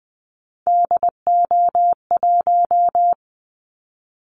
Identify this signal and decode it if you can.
Morse code, heard as a keyed tone, about 20 wpm, 700 Hz. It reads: DO1
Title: Callsign Prefixes for Top 50 DXCC - 20wpm